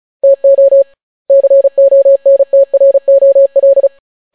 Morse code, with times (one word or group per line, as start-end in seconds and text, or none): 0.2-0.8 TO
1.3-3.9 CONTROL